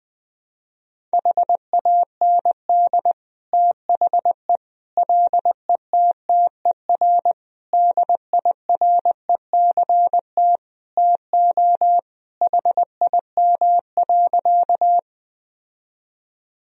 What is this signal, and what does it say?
Morse code, heard as a keyed tone, about 20 wpm, 700 Hz. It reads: HAND THE LETTER DIRECT TO HIM.
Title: Sentences from Top 1000 Words - Review Part 2 - 20wpm